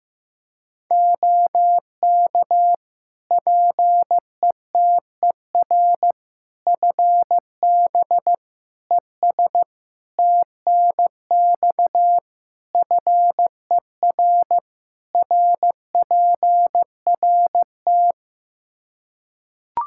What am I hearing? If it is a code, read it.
Morse code, heard as a keyed tone, about 15 wpm, 700 Hz. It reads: OK PETER FB ES TNX FER RPRT
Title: K1USN SST QSO Practice 15wpm